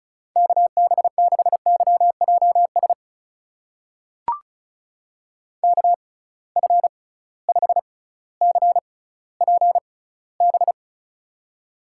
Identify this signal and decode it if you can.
Morse code, heard as a keyed tone, about 36 wpm, spaced as for 18 wpm, 700 Hz. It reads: KB6YJS KF5CPB
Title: USA Call Signs - 6x Character Spacing - 35wpm